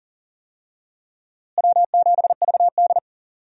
Morse code, heard as a keyed tone, about 40 wpm, 700 Hz. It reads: W7VD
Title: USA Call Signs - 4 Characters - Repeated 2x - 40wpm